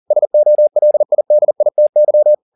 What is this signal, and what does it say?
Morse code, heard as a keyed tone, about 40 wpm, 600 Hz. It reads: SOLIDITY
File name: the old man and the sea 2 40wpm